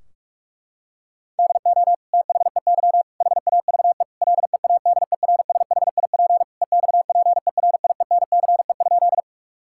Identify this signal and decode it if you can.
Morse code, heard as a keyed tone, about 46 wpm, 700 Hz. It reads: DO THEY HAVE LEADERSHIP EXPERIENCE?
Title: Sentences from Top 1600 Words 45wpm